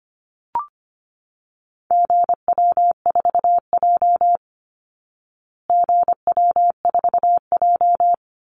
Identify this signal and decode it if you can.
Morse code, heard as a keyed tone, about 25 wpm, 700 Hz. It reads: GW4J GW4J
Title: Call Signs Repeated 2x - 25wpm